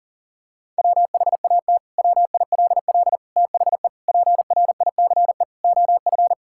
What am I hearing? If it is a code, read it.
Morse code, heard as a keyed tone, about 40 wpm, 700 Hz. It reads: WHAT WILL THE PRICE OF
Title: Sentences from Top 1000 Words - Review Part 5 - 40wpm